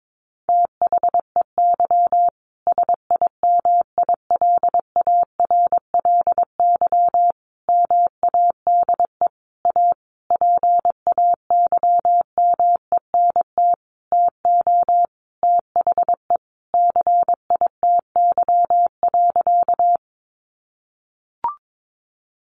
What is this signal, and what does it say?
Morse code, heard as a keyed tone, about 22 wpm, 700 Hz. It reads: THEY SIMILARLY MADE A PAYMENT TO THE CITY.
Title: Sentences from Top 1400 Words 22wpm